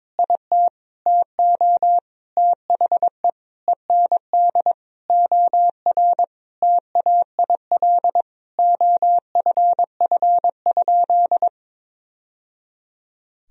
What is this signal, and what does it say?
Morse code, heard as a keyed tone, about 22 wpm, 700 Hz. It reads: IT TO THE END OR TAIL OFF?
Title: Sentences from Top 1000 Words - Review Part 3 - 22wpm